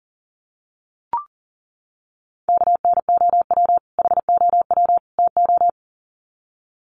Morse code, heard as a keyed tone, about 40 wpm, 700 Hz. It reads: KNOW HOW TO
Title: Sets of 3 Words 40wpm